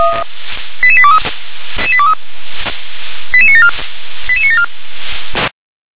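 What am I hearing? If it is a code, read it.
Five-tone selective call (ZVEI), heard as quick runs of five tones, tones of about 70 ms. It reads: 88823 88974 88974
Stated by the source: ZVEI five-tone selective call received on an SDR.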